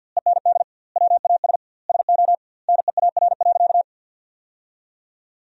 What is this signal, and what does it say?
Morse code, heard as a keyed tone, about 49 wpm, 700 Hz. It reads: END WAS SO NEAR.
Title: 2x Word-Spacing - Sentences from Top 300 Words - 50wpm